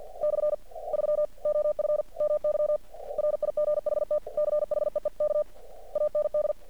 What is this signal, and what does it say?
Morse code, heard as a keyed tone, about 48 wpm, 600 Hz. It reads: /3 OF MY RIGHT CHEEK AND
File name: archived / CRSnet